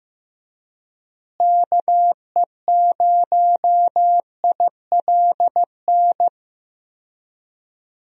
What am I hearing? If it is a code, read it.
Morse code, heard as a keyed tone, about 15 wpm, 700 Hz. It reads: KE0ILN
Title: USA Call Signs - 6 Characters - Repeated 2x - 15wpm